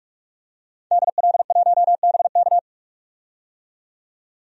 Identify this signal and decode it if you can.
Morse code, heard as a keyed tone, about 46 wpm, 700 Hz. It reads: DL1BK
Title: DXCC-100 Call Signs - 5 Characters - 45wpm